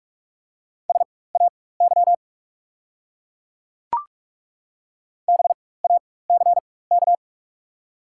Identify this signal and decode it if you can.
Morse code, heard as a keyed tone, about 42 wpm, spaced as for 21 wpm, 690 Hz. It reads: SAY BACK
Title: 4x Character Spacing - Top 300 Words - 45wpm